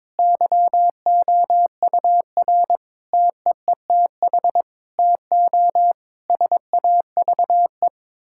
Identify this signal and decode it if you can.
Morse code, heard as a keyed tone, about 22 wpm, 700 Hz. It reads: YOUR TEETH TO SAVE
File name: Difficult Sentences with Hints - 22wpm